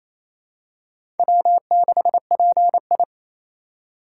Morse code, heard as a keyed tone, about 28 wpm, 700 Hz. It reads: W6PI
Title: USA Call Signs - 2x Character Spacing - 28wpm